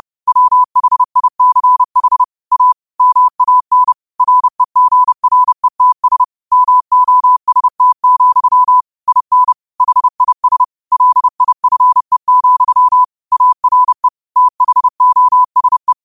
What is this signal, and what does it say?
Morse code, heard as a keyed tone, about 30 wpm, 1000 Hz. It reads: WHICH A MAN REGRETS MOST, IN HIS LIFE, ARE THOSE